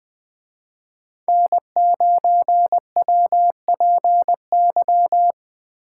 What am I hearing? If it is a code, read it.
Morse code, heard as a keyed tone, about 23 wpm, 700 Hz. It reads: N9WPY
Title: Speed Racing - USA Call Signs 20wpm